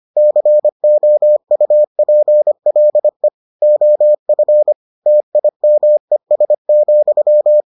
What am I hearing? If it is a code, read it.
Morse code, heard as a keyed tone, about 25 wpm, 600 Hz. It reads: COUPLE OF TIMES,